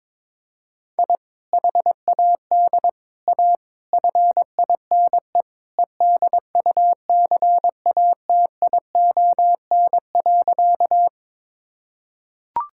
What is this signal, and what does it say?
Morse code, heard as a keyed tone, about 22 wpm, 700 Hz. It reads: I HAD A FINE EDUCATION.
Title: Sentences from Top 1000 Words - Review Part 6 - 22wpm